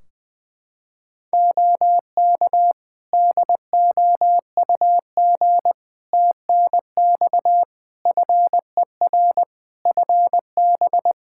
Morse code, heard as a keyed tone, about 20 wpm, 700 Hz. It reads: OK DOUG TNX FER FB
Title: Practice QSO - Exchange 4 - Ending - 20wpm